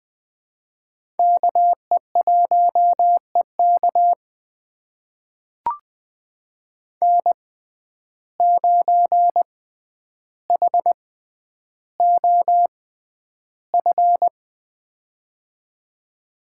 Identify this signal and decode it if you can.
Morse code, heard as a keyed tone, about 20 wpm, spaced as for 10 wpm, 700 Hz. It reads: KE1EK N9HOF
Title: USA Call Signs - 5 Characters - 6x Character Spacing - 20wpm